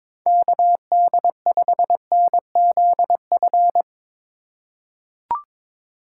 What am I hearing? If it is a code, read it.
Morse code, heard as a keyed tone, about 23 wpm, 700 Hz. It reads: KD5NZF
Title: USA Call Signs - 6 Characters - 3x Character Spacing - 22wpm